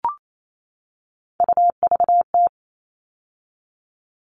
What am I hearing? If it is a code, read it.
Morse code, heard as a keyed tone, about 28 wpm, 700 Hz. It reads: UVT
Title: All Cut Numbers - 3-Digit 28wpm